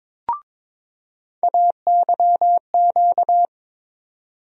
Morse code, heard as a keyed tone, about 22 wpm, 700 Hz. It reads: AYQ
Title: USA Call Sign Postfixes - 3 Letters 22wpm